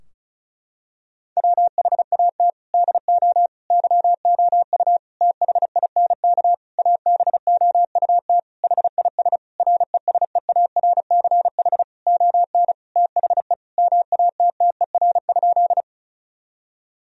Morse code, heard as a keyed tone, about 36 wpm, 700 Hz. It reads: WHAT DO YOU THINK ABOUT HIS RESEARCH ON THE MATTER?
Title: Sentences from Top 1000 Words - Review Part 4 - 35wpm